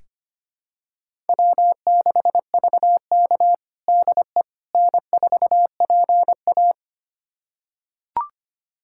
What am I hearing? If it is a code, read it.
Morse code, heard as a keyed tone, about 25 wpm, 700 Hz. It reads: W6VK DE N4PA
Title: USA Callsign DE Callsign - 4 Characters - 25wpm